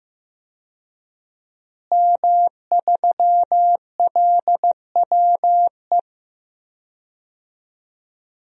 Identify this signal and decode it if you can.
Morse code, heard as a keyed tone, about 15 wpm, 700 Hz. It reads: M3LWE
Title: DXCC-100 Call Signs - 5 Characters - Repeated 2x - 15wpm